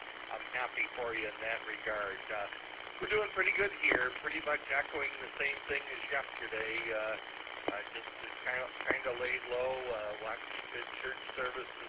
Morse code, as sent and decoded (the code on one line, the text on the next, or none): none